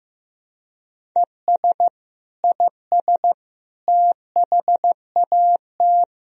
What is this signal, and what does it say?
Morse code, heard as a keyed tone, about 15 wpm, 700 Hz. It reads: ES IS THAT
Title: Sentences from Top 100 Words and Abbreviations 15wpm